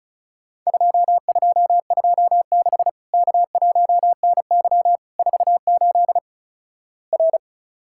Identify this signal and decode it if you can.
Morse code, heard as a keyed tone, about 36 wpm, 700 Hz. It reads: 2226 K1NY 48 R
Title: Worked All Europe DX Contest - QTC Exchange - 35wpm